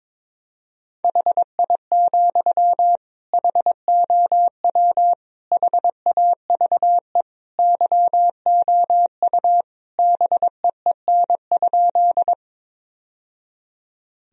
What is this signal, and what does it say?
Morse code, heard as a keyed tone, about 22 wpm, 700 Hz. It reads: HI, HOW HAVE YOU BEEN?
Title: Sentences from Top 1000 Words - Review Part 3 - 22wpm